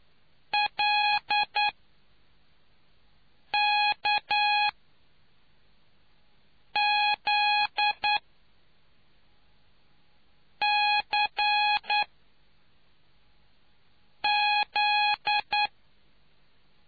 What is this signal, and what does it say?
Morse code, heard as a keyed tone, about 9 wpm, spaced as for 3 wpm, 830 Hz. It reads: LKZCZ